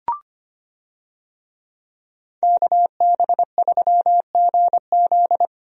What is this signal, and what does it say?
Morse code, heard as a keyed tone, about 25 wpm, 700 Hz. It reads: KB3GZ
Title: USA Call Signs - 5 Characters - Repeated 2x - 25wpm